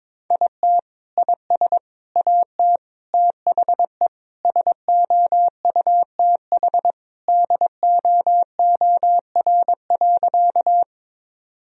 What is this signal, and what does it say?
Morse code, heard as a keyed tone, about 22 wpm, 700 Hz. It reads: IT IS AT THE SOUTH DOOR.
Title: Sentences from Top 1000 Words - Review Part 2 - 22wpm